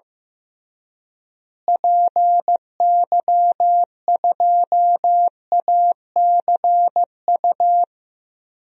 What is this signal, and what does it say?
Morse code, heard as a keyed tone, about 15 wpm, 700 Hz. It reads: PY2ACU
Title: DXCC-100 Call Signs - 6 Characters - Repeated 3x - 15wpm